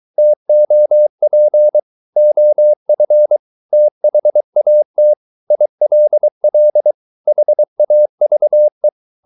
Morse code, heard as a keyed tone, about 23 wpm, 600 Hz. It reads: TOP OF THAT ILL HAVE